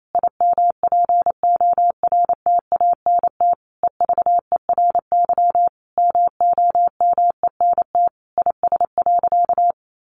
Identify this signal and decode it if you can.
Morse code, heard as a keyed tone, about 28 wpm, 700 Hz. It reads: IMPORTANT EVERY MOMENT IS.
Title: Sentences from Top 1000 Words 28wpm